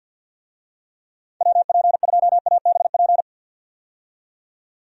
Morse code, heard as a keyed tone, about 52 wpm, 700 Hz. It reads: WP2ABP
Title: USA Call Signs - 6 Characters - 50wpm